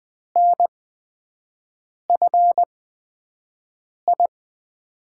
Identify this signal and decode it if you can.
Morse code, heard as a keyed tone, about 20 wpm, spaced as for 5 wpm, 700 Hz. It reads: NFI
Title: USA Call Signs - 5 Characters - 8x Character Spacing - 20wpm